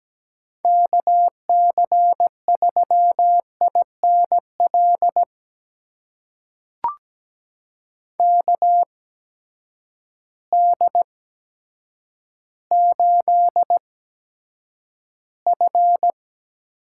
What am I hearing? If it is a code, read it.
Morse code, heard as a keyed tone, about 17 wpm, spaced as for 8 wpm, 700 Hz. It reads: KC3INL KD8F
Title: USA Call Signs - 6 Characters - 8x Character Spacing - 17wpm